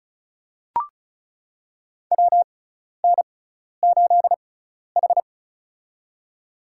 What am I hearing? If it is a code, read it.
Morse code, heard as a keyed tone, about 34 wpm, spaced as for 12 wpm, 700 Hz. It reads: WN8H
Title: USA Call Signs - 4 Characters - 6x Character Spacing - 35wpm